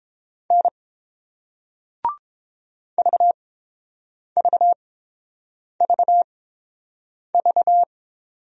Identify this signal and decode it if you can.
Morse code, heard as a keyed tone, about 30 wpm, spaced as for 7 wpm, 700 Hz. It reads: N VVVV